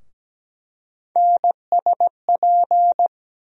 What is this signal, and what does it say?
Morse code, heard as a keyed tone, about 17 wpm, 700 Hz. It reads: NSP
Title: USA Call Sign Postfixes - 3 Letters 17wpm